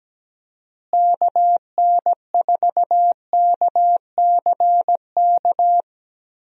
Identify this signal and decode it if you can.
Morse code, heard as a keyed tone, about 17 wpm, 700 Hz. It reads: KN4KCK
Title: USA Call Signs - 6 Characters - Repeated 2x - 17wpm